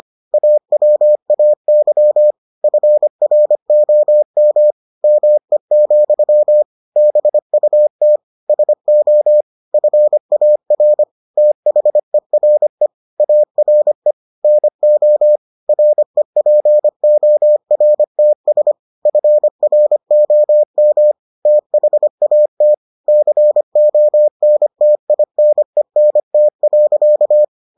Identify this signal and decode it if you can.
Morse code, heard as a keyed tone, about 25 wpm, 600 Hz. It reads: AWAY FROM ME, BUT SO FAR THERE ARE NO REPORTS FROM THAT CONTINENT.